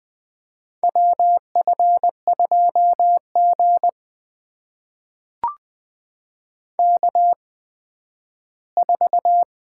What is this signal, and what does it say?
Morse code, heard as a keyed tone, about 20 wpm, spaced as for 12 wpm, 700 Hz. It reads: WF2G K4
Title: USA Call Signs - 8x Character Spacing - 20wpm